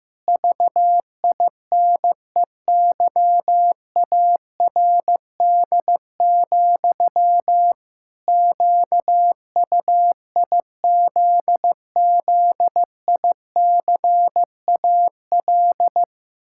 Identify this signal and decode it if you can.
Morse code, heard as a keyed tone, about 15 wpm, 700 Hz. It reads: VINEYARD, QUIZZICAL
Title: Pangram Challenge v2 - 15wpm